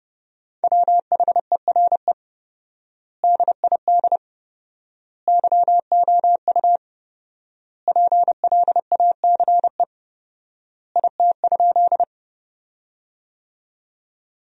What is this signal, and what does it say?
Morse code, heard as a keyed tone, about 30 wpm, 700 Hz. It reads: WHERE DID YOU PLACE IT?
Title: 4x Word-Spacing - Sentences from Top 200 Words - 30wpm